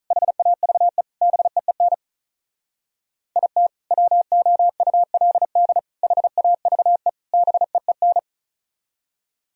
Morse code, heard as a keyed tone, about 44 wpm, 700 Hz. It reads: HAVE BEEN IT WOULD HAVE BEEN